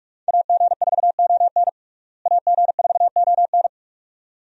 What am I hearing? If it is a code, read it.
Morse code, heard as a keyed tone, about 45 wpm, 700 Hz. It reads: AG4ON AG4ON
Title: USA Call Signs - 5 Characters - Repeated 2x - 45wpm